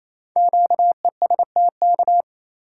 Morse code, heard as a keyed tone, about 29 wpm, 700 Hz. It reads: QESTK
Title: Code Groups - 5 Letters - 28wpm